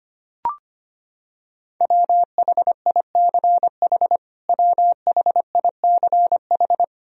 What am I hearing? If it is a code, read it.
Morse code, heard as a keyed tone, about 25 wpm, 700 Hz. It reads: WHICH WHICH
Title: Top 100 Words Repeated 25wpm